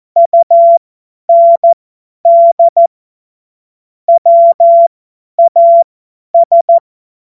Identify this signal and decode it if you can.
Morse code, heard as a keyed tone, about 14 wpm, spaced as for 10 wpm, 680 Hz. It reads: UND WAS
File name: Morsekurs / Witze / Witz 21. September 2021, 50 F70